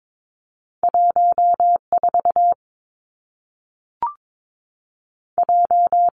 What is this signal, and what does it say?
Morse code, heard as a keyed tone, about 22 wpm, 700 Hz. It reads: 1 4 J
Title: Two Challenging Letter-Number Combinations 22wpm